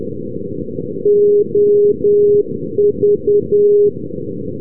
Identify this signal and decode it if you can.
Morse code, heard as a keyed tone, about 10 wpm, 410 Hz. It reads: OV
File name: europe / s-279 ndb salthamn-visby ov-351 np-s-linkoeping- -rb